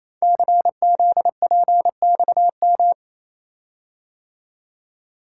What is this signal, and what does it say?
Morse code, heard as a keyed tone, about 27 wpm, 700 Hz. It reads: CZPXM
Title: Code Groups - 5 Letters - 28wpm